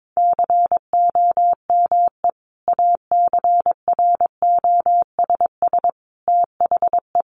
Morse code, heard as a keyed tone, about 22 wpm, 700 Hz. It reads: COME ACROSS THE
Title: Sentences from Top 700 Words 22wpm